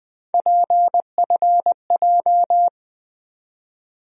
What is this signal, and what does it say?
Morse code, heard as a keyed tone, about 20 wpm, 700 Hz. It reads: PFJ
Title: USA Call Sign Postfixes - 3 Letters 20wpm